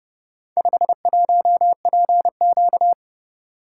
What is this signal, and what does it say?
Morse code, heard as a keyed tone, about 30 wpm, 700 Hz. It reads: 51PQ